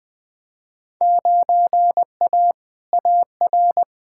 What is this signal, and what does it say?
Morse code, heard as a keyed tone, about 21 wpm, 700 Hz. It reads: 9A AR
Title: Field Day Exchanges Encore - 20wpm